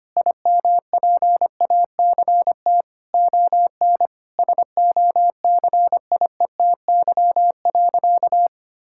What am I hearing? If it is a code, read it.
Morse code, heard as a keyed tone, about 25 wpm, 700 Hz. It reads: IMPACT ON SOCIETY.
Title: Sentences from Top 1000 Words - Review Part 6 - 25wpm